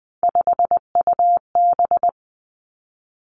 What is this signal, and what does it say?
Morse code, heard as a keyed tone, about 20 wpm, 700 Hz. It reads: 5UB
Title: Three Challenging Letter-Number Combinations 20wpm